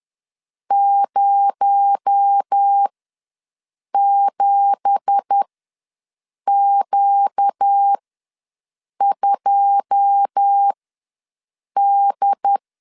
Morse code, heard as a keyed tone, about 10 wpm, spaced as for 6 wpm, 800 Hz. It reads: 07Q2D
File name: LEZIONI / 9 2867 ws 10wpm 4215